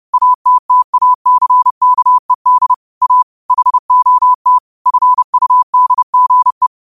Morse code, heard as a keyed tone, about 30 wpm, 1000 Hz. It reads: ATTACKED A HOT FUDGE
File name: qotdfiles / 30wpm